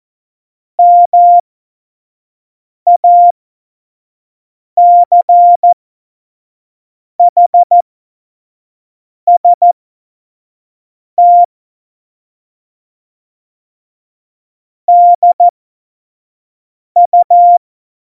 Morse code, heard as a keyed tone, about 14 wpm, spaced as for 5 wpm, 700 Hz. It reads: MACHST DU
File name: Morsekurs / Witze / Witz 24.Juli 2018, 25 F70